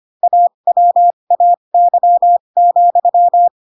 Morse code, heard as a keyed tone, about 25 wpm, spaced as for 22 wpm, 700 Hz. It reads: AWAY,
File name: Pinocchio / Pinocchio ch7 18 25 ik1hge